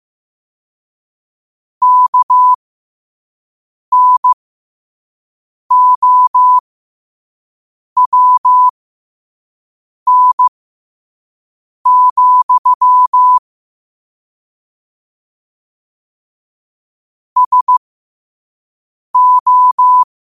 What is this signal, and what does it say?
Morse code, heard as a keyed tone, about 15 wpm, spaced as for 5 wpm, 1000 Hz. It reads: KNOWN, SO
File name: 5wpm